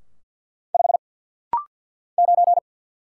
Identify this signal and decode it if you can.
Morse code, heard as a keyed tone, about 50 wpm, 700 Hz. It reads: 5 9
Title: Single Letter-Number - Rapid-Fire 50wpm